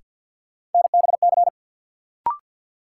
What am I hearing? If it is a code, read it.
Morse code, heard as a keyed tone, about 48 wpm, 700 Hz. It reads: NBC
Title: Sets of 5 Words Encore - New Word Drill - 50wpm